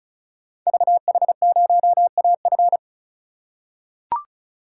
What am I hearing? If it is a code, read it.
Morse code, heard as a keyed tone, about 36 wpm, 700 Hz. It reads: VH0AF
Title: Code Groups - 5 Letter:Numbers - 35wpm